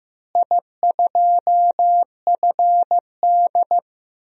Morse code, heard as a keyed tone, about 15 wpm, 700 Hz. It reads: I2FD
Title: DXCC-100 Call Signs - 4 Characters - Repeated 3x - 15wpm